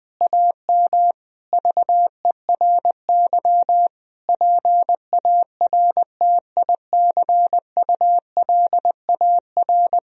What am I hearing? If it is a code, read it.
Morse code, heard as a keyed tone, about 20 wpm, 700 Hz. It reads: AM VERY PARTICULAR
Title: Sentences from Top 1000 Words - Review Part 4 - 20wpm